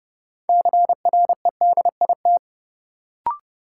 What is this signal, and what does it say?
Morse code, heard as a keyed tone, about 30 wpm, 700 Hz. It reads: CREDIT
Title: Sentences from Top 1000 Words - New Word Drill - 30wpm